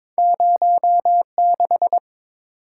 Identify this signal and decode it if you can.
Morse code, heard as a keyed tone, about 30 wpm, spaced as for 26 wpm, 700 Hz. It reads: 06
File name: Speed-Racing - Two Letter-Number Combinations - 22wpm